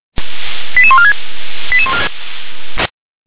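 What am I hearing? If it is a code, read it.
Five-tone selective call (ZVEI), heard as quick runs of five tones, tones of about 70 ms. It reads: 88146
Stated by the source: Received on an SDR; ZVEI five-tone call.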